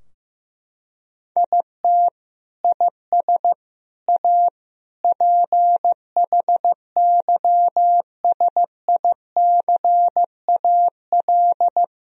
Morse code, heard as a keyed tone, about 16 wpm, 700 Hz. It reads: IT IS A PHYSICAL
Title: Sentences from Top 1000 Words - Review Part 4 - 15wpm